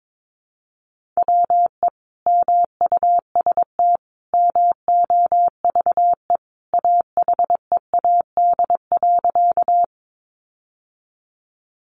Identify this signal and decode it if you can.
Morse code, heard as a keyed tone, about 22 wpm, 700 Hz. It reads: WE MUST MOVE AHEAD.
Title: Sentences from Top 1100 Words 22wpm